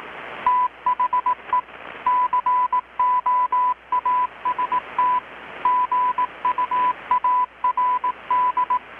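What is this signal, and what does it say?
Morse code, heard as a keyed tone, about 18 wpm, 1000 Hz. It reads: THE COAST GUARD
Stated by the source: Morse from KPH's Night of Nights XX maritime radiotelegraphy event, received in Chiapas, Mexico about 3,800 km away on a Kenwood R-600 with a grounded 10 m long-wire antenna.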